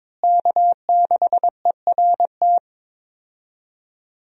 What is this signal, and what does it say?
Morse code, heard as a keyed tone, about 22 wpm, 700 Hz. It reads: K6ERT
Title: USA Call Signs - 5 Characters - Repeated 3x - 22wpm